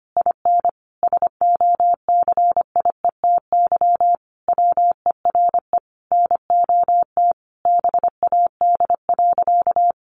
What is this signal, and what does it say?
Morse code, heard as a keyed tone, about 25 wpm, 700 Hz. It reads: IN SOCIETY WERE NOT BAD.
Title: Sentences from Top 900 Words 25wpm